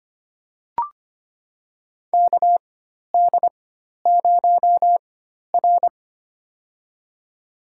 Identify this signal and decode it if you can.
Morse code, heard as a keyed tone, about 25 wpm, spaced as for 12 wpm, 690 Hz. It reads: KD0R
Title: USA Call Signs - 4 Characters - 4x Character Spacing - 25wpm